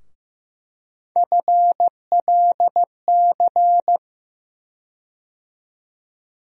Morse code, heard as a keyed tone, about 15 wpm, 700 Hz. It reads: FLC
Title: USA Call Sign Postfixes - 3 Letters 15wpm